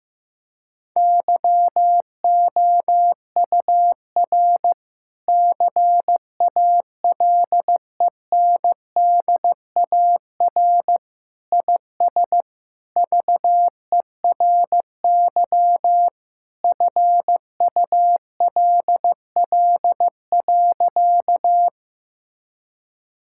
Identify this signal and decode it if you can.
Morse code, heard as a keyed tone, about 16 wpm, 700 Hz. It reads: YOUR CALENDAR IS VERY FULL.
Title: Sentences from Top 1600 Words 15wpm